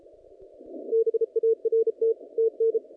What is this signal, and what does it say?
Morse code, heard as a keyed tone, about 33 wpm, 450 Hz. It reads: BART TN
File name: contests / naqp-cw-k5tr-op-k5pi- -0300